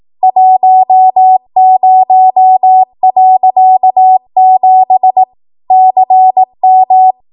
Morse code, heard as a keyed tone, about 18 wpm, 750 Hz. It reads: 10.7 CM
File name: Morse / 18WPM